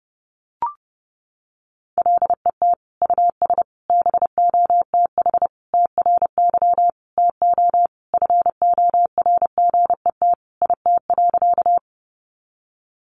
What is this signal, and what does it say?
Morse code, heard as a keyed tone, about 30 wpm, 700 Hz. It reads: LET US BOTH TRY TO FORGET IT.
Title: Sentences from Top 1200 Words 30wpm